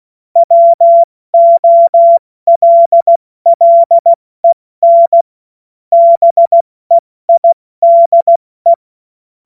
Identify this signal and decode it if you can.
Morse code, heard as a keyed tone, about 16 wpm, 680 Hz. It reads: WOLLEN BEIDE
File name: Morsekurs / Witze / Witz 21. März 2023 70 F80